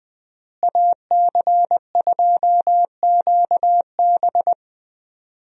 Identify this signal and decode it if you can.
Morse code, heard as a keyed tone, about 21 wpm, 700 Hz. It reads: AC2QB
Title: USA Call Signs - 5 Characters - Repeated 2x - 20wpm